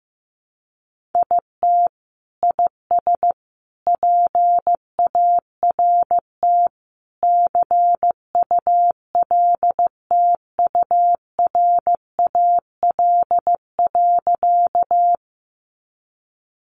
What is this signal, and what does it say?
Morse code, heard as a keyed tone, about 15 wpm, 700 Hz. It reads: IT IS PART CULTURAL.
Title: Sentences from Top 1500 Words 15wpm